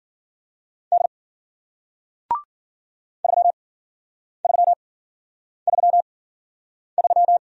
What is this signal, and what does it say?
Morse code, heard as a keyed tone, about 53 wpm, spaced as for 9 wpm, 700 Hz. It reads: D 3333